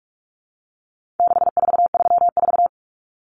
Morse code, heard as a keyed tone, about 44 wpm, 700 Hz. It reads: 6434
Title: Numbers - 4-Digit 45wpm